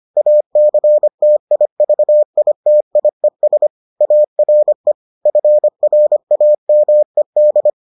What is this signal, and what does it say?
Morse code, heard as a keyed tone, about 25 wpm, 600 Hz. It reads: ACTIVITIES ARE FRAMED